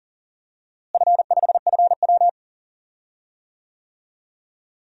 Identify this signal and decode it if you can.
Morse code, heard as a keyed tone, about 40 wpm, 700 Hz. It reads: F5FW
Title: DXCC-100 Call Signs - 4 Characters - Repeated 3x - 40wpm